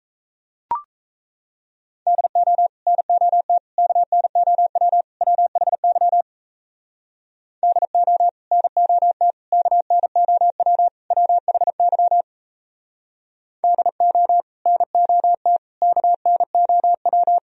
Morse code, heard as a keyed tone, about 37 wpm, 700 Hz. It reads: DO NOT KNOW WHY DO NOT KNOW WHY DO NOT KNOW